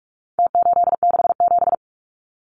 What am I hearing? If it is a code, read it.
Morse code, heard as a keyed tone, about 44 wpm, 700 Hz. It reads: T867
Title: Numbers - Contest Style - 4-Digit 45wpm